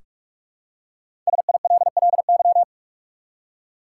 Morse code, heard as a keyed tone, about 47 wpm, 700 Hz. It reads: SILLY
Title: Five Letter Words - 45wpm